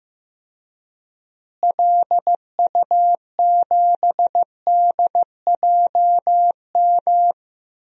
Morse code, heard as a keyed tone, about 15 wpm, 700 Hz. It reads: LU7DJM
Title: DXCC-100 Call Signs - 6 Characters - Repeated 2x - 15wpm